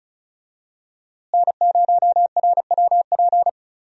Morse code, heard as a keyed tone, about 36 wpm, 700 Hz. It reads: N0RWP
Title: USA Call Signs - 5 Characters - Repeated 2x - 35wpm